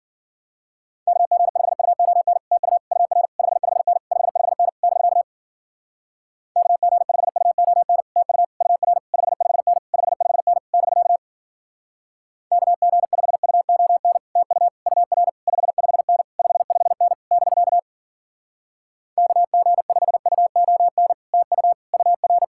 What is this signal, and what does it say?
Morse code, heard as a keyed tone, about 50 wpm, 700 Hz. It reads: KG5UON TU UR 55N 55N <BK> KG5UON TU UR 55N 55N <BK> KG5UON TU UR 55N 55N <BK> KG5UON TU UR